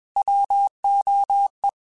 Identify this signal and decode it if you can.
Morse code, heard as a keyed tone, about 21 wpm, 800 Hz. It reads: WOE
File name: Alphabetical / Alphabetical mp3s / Ws @20